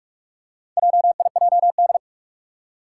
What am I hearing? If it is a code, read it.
Morse code, heard as a keyed tone, about 44 wpm, 700 Hz. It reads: JIJD
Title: Four Challenging Letter-Number Combinations 45wpm